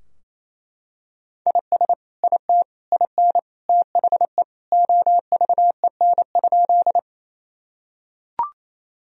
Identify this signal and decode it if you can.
Morse code, heard as a keyed tone, about 28 wpm, 700 Hz. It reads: IS IT IN THE OVEN?
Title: Sentences from Top 1000 Words - Review Part 4 - 28wpm